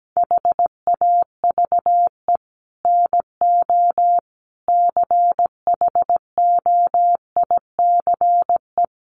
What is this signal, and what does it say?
Morse code, heard as a keyed tone, about 17 wpm, 700 Hz. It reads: HAVE NO CHOICE
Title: Sets of 5 Words Encore 17wpm